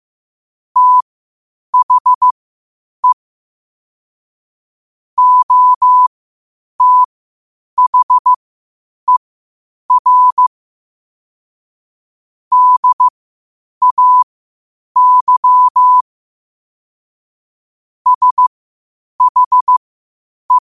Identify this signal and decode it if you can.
Morse code, heard as a keyed tone, about 15 wpm, spaced as for 8 wpm, 1000 Hz. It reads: THE OTHER DAY SHE
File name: qotdfiles / 8wpm